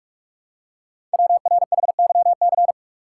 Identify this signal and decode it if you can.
Morse code, heard as a keyed tone, about 44 wpm, 690 Hz. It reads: WRHYC
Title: Code Groups - 5 Letters - 45wpm